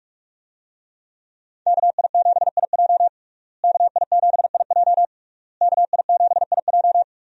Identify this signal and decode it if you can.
Morse code, heard as a keyed tone, about 41 wpm, 700 Hz. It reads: KI7IJ KI7IJ KI7IJ
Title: USA Call Signs - 5 Characters - Repeated 3x - 45wpm